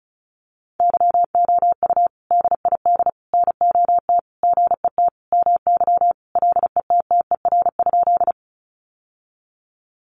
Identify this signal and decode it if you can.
Morse code, heard as a keyed tone, about 34 wpm, 700 Hz. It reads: YOU DID NOT GET MY LETTER?
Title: Sentences from Top 600 Words 35wpm